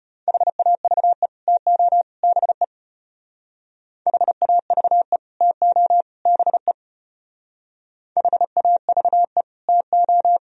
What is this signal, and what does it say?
Morse code, heard as a keyed tone, about 30 wpm, 700 Hz. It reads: HAVE TO BE HAVE TO BE HAVE TO
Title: Speed-Racing - Sets of 3 Words 25wpm